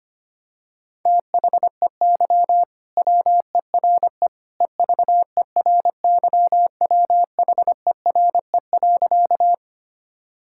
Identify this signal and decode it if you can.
Morse code, heard as a keyed tone, about 25 wpm, 700 Hz. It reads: THEY WERE EVERYWHERE.
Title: Sentences from Top 1600 Words 25wpm